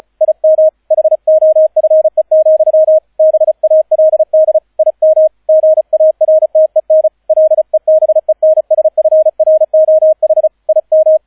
Morse code, heard as a keyed tone, about 33 wpm, 630 Hz, with radio noise. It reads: IM HOFE, BALD IM GARTEN LEBENSFROH IM